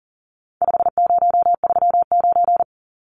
Morse code, heard as a keyed tone, about 40 wpm, 700 Hz. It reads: 5039
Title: Numbers - 4-Digit 40wpm